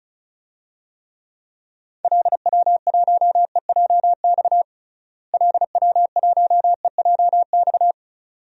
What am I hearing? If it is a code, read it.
Morse code, heard as a keyed tone, about 34 wpm, 700 Hz. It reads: LW1EJX LW1EJX
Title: DXCC-100 Call Signs - 6 Characters - Repeated 2x - 35wpm